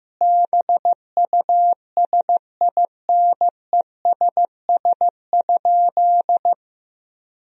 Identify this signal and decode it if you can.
Morse code, heard as a keyed tone, about 15 wpm, 700 Hz. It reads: BUSINESS?
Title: Sentences from Top 1000 Words - Review Part 4 - 15wpm